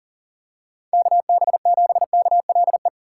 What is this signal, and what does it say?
Morse code, heard as a keyed tone, about 40 wpm, 700 Hz. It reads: KB7KLE
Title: USA Call Signs - 6 Characters - 3x Character Spacing - 40wpm